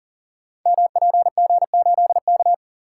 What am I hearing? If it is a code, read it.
Morse code, heard as a keyed tone, about 42 wpm, 700 Hz. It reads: MPG8K
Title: Code Groups - 5 Letter:Numbers - 40wpm